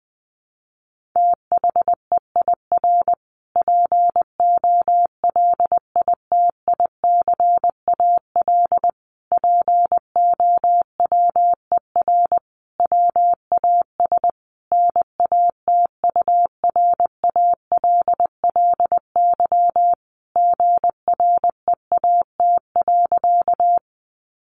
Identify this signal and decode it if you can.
Morse code, heard as a keyed tone, about 20 wpm, 700 Hz. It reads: THEIR POLITICAL POWER WAS NATURALLY GREAT.
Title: Sentences from Top 1300 Words 20wpm